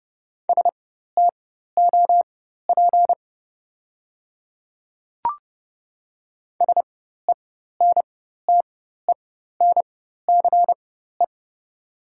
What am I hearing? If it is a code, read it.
Morse code, heard as a keyed tone, about 30 wpm, spaced as for 14 wpm, 690 Hz. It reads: STOP SENTENCE
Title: 4x Character Spacing - Top 300 Words - 30wpm